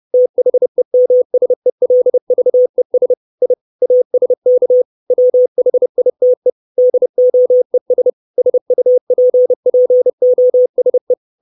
Morse code, heard as a keyed tone, about 30 wpm, 500 Hz. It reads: THEMSELVES I ASK WHITE DOES SUPPOSE
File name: 632 Words 30WPM